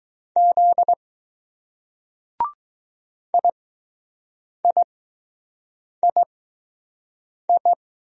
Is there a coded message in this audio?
Yes: Morse code from 0.4 to 7.7 s, heard as a keyed tone, about 20 wpm, spaced as for 6 wpm, 700 Hz. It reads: Z IIII